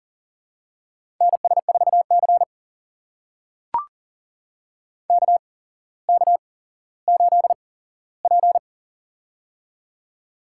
Morse code, heard as a keyed tone, about 42 wpm, spaced as for 17 wpm, 700 Hz. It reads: NS4C KK8P
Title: USA Call Signs - 4 Characters - 8x Character Spacing - 40wpm